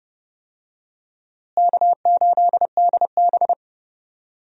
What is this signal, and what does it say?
Morse code, heard as a keyed tone, about 30 wpm, 700 Hz. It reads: K8DB
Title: USA Call Signs - 4 Characters - Repeated 2x - 30wpm